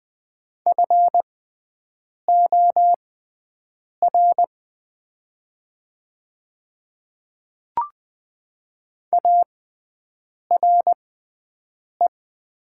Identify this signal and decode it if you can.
Morse code, heard as a keyed tone, about 20 wpm, spaced as for 7 wpm, 700 Hz. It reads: FOR ARE
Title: 6x Character Spacing - Top 100 Words - 20wpm